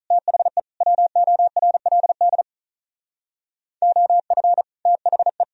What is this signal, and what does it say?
Morse code, heard as a keyed tone, about 46 wpm, 700 Hz. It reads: THE WORLD OF THE